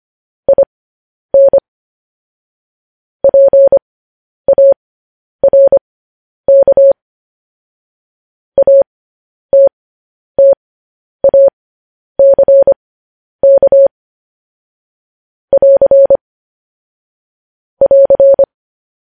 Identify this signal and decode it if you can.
Morse code, heard as a keyed tone, about 25 wpm, spaced as for 10 wpm, 550 Hz. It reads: IN PARK ATTACK <AR> <AR>
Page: news headlines in morse code at 10 wpm